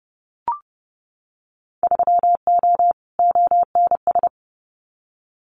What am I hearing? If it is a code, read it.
Morse code, heard as a keyed tone, about 30 wpm, 700 Hz. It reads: 3O ONS
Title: Winter Field Day Exchanges Encore 30wpm